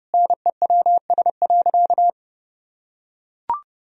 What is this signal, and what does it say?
Morse code, heard as a keyed tone, about 30 wpm, 700 Hz. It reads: NEWS.
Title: Sentences from Top 1000 Words - Review Part 3 - 30wpm